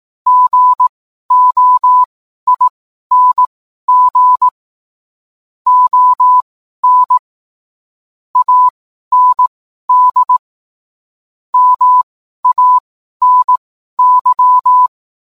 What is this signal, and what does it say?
Morse code, heard as a keyed tone, about 18 wpm, spaced as for 12 wpm, 1000 Hz. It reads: GOING ON AND MANY